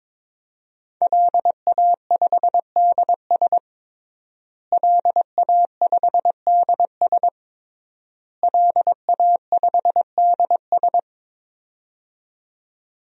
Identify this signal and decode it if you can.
Morse code, heard as a keyed tone, about 22 wpm, 700 Hz. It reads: LA5DS LA5DS LA5DS
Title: DXCC-100 Call Signs - 5 Characters - Repeated 3x - 22wpm